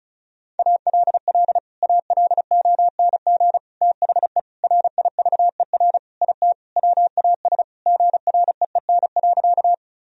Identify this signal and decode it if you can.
Morse code, heard as a keyed tone, about 36 wpm, 700 Hz. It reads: ALL ALONG THE RIVER IT WAS GREEN.
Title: Sentences from Top 1000 Words - Review Part 2 - 35wpm